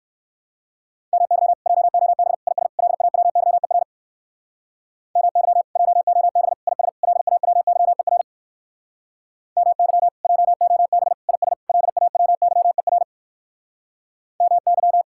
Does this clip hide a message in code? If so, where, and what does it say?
Yes: Morse code from 1.1 to 15.0 s, heard as a keyed tone, about 57 wpm, 700 Hz. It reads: MY JOB IS LAWYER MY JOB IS LAWYER MY JOB IS LAWYER MY